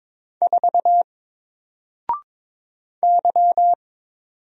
Morse code, heard as a keyed tone, about 22 wpm, 700 Hz. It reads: 4 Y
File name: Single-Letter-Number-22wpm